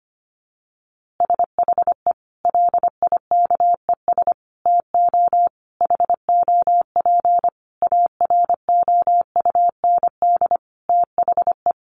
Morse code, heard as a keyed tone, about 25 wpm, 700 Hz. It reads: SHE LIKES TO HOP AROUND THE
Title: Easy Phrases 25wpm